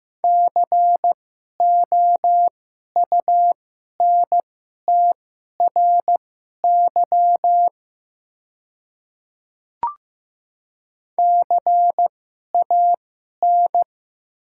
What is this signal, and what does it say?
Morse code, heard as a keyed tone, about 15 wpm, spaced as for 11 wpm, 700 Hz. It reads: COUNTRY CAN
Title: 2x Character Spacing - Top 300 Words - 15wpm